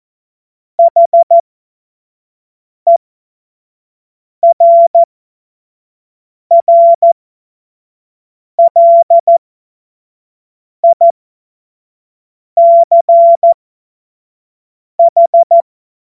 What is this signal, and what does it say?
Morse code, heard as a keyed tone, about 14 wpm, spaced as for 5 wpm, 680 Hz. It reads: HERRLICH
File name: Witz-23-September-2025-25-F70